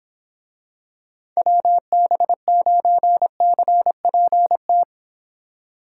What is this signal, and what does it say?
Morse code, heard as a keyed tone, about 26 wpm, 700 Hz. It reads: WB9CPT